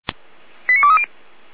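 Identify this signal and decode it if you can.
Five-tone selective call (ZVEI), heard as quick runs of five tones, tones of about 80 ms. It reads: 89230
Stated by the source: ZVEI five-tone selective call 89206 received by SDR.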